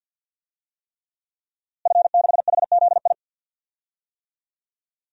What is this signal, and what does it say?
Morse code, heard as a keyed tone, about 50 wpm, 700 Hz. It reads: F6HZI
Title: DXCC-100 Call Signs - 5 Characters - Repeated 3x - 50wpm